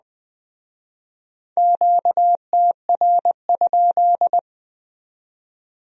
Morse code, heard as a keyed tone, about 21 wpm, 700 Hz. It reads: QTR?
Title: Speed-Racing Q-Codes - 17wpm